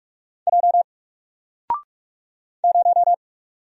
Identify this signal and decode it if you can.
Morse code, heard as a keyed tone, about 44 wpm, 700 Hz. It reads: J 0
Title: Single Character (Letter:Number:Punctuation:Prosign) 45wpm